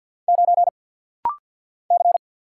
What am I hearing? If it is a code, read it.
Morse code, heard as a keyed tone, about 51 wpm, 690 Hz. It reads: 9 C
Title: Single Letter-Number - Rapid-Fire 50wpm